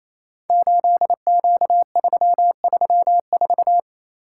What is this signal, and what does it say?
Morse code, heard as a keyed tone, about 27 wpm, 700 Hz. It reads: 8Q334
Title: Code Groups - 5 Letter:Numbers - 28wpm